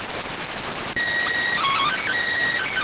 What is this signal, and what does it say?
SSTV (slow-scan television), PD 290.